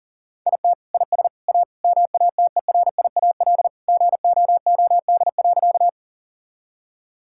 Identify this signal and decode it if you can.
Morse code, heard as a keyed tone, about 37 wpm, 700 Hz. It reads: IT IS A MATERIAL GOOD.
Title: Sentences from Top 1000 Words - Review Part 5 - 40wpm